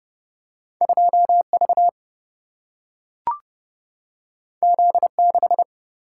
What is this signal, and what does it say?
Morse code, heard as a keyed tone, about 30 wpm, 700 Hz. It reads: 2 V Z 6
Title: Single-Letter Single-Number 30wpm